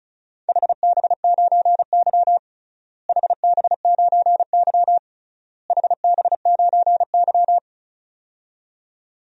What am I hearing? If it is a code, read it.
Morse code, heard as a keyed tone, about 34 wpm, 700 Hz. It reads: HB9Y HB9Y HB9Y
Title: DXCC-100 Call Signs - 4 Characters - Repeated 3x - 35wpm